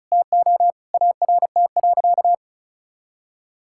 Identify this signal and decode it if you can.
Morse code, heard as a keyed tone, about 34 wpm, 700 Hz. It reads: TO ART.
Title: Sentences from Top 1000 Words - Review Part 6 - 35wpm